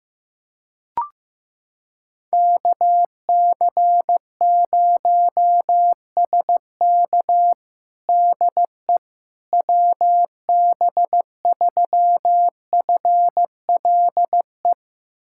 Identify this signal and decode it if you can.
Morse code, heard as a keyed tone, about 16 wpm, 700 Hz. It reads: KC0SK DE WB3FLE
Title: USA Callsign DE Callsign 15wpm